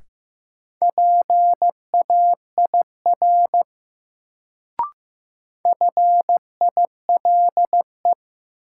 Morse code, heard as a keyed tone, about 15 wpm, 700 Hz. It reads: PAIR FILE
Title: Sentences from Top 1200 Words - New Word Drill Alt - 15wpm